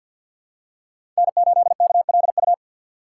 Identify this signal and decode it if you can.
Morse code, heard as a keyed tone, about 47 wpm, 700 Hz. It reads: N8KLU
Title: USA Call Signs - 5 Characters - 3x Character Spacing - 50wpm